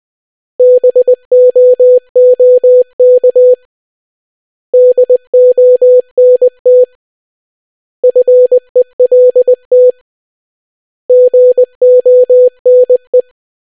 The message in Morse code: BOOK DONT FELT GONE

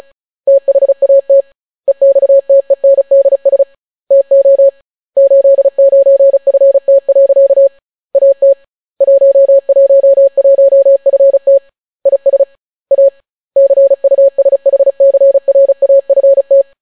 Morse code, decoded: THAT EXTENDS TO 89FT. AT 111FT IS A CUSHCRAFT

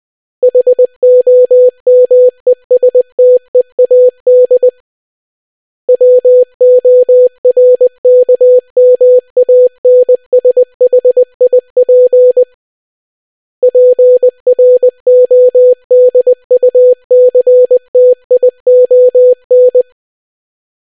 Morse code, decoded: HOMESTEAD WORKMANSHIP PRODUCTION

